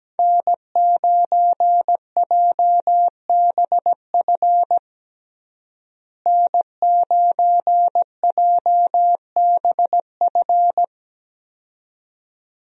Morse code, heard as a keyed tone, about 17 wpm, 700 Hz. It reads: N9JBF N9JBF
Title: USA Call Signs - 5 Characters - Repeated 2x - 17wpm